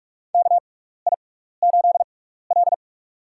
Morse code, heard as a keyed tone, about 44 wpm, spaced as for 15 wpm, 700 Hz. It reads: KI8L
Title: USA Call Signs - 4 Characters - 6x Character Spacing - 45wpm